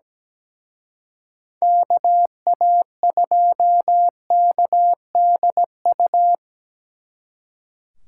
Morse code, heard as a keyed tone, about 20 wpm, 700 Hz. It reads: KA2KDU
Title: Speed Racing - USA 2x3 Call Signs 17wpm